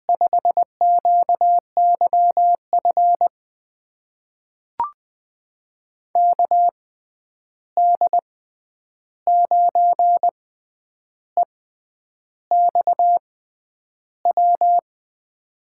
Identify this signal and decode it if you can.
Morse code, heard as a keyed tone, about 20 wpm, spaced as for 9 wpm, 700 Hz. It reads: 5QYF KD9EXW
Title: USA Call Signs - 6 Characters - 6x Character Spacing - 20wpm